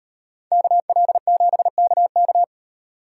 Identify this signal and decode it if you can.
Morse code, heard as a keyed tone, about 38 wpm, 700 Hz. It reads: KL7KK